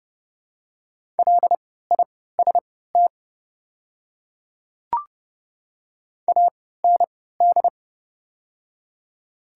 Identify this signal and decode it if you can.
Morse code, heard as a keyed tone, about 31 wpm, spaced as for 17 wpm, 700 Hz. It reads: LIST AND
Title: 3x Character Spacing - Top 500 Words - 30wpm